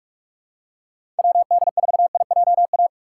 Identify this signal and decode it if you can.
Morse code, heard as a keyed tone, about 46 wpm, 700 Hz. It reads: WD4IJA
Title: USA Call Signs - 6 Characters - 3x Character Spacing - 45wpm